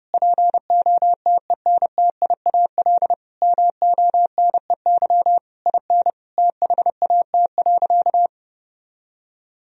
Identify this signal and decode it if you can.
Morse code, heard as a keyed tone, about 32 wpm, 700 Hz. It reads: POTENTIAL MONEY IN THAT.
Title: Sentences from Top 1000 Words - Review Part 4 - 30wpm